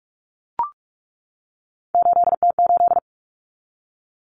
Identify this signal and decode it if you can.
Morse code, heard as a keyed tone, about 44 wpm, 700 Hz. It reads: 8T8
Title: Common Cut Numbers - 3-Digit 45wpm